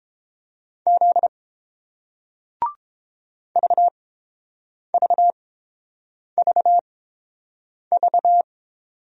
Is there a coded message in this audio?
Yes: Morse code from 0.9 to 8.4 s, heard as a keyed tone, about 31 wpm, spaced as for 7 wpm, 690 Hz. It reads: Z VVVV